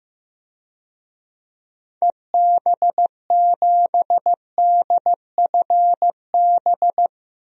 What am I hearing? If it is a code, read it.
Morse code, heard as a keyed tone, about 15 wpm, 700 Hz. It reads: EB7DFB
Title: DXCC-100 Call Signs - 6 Characters - 15wpm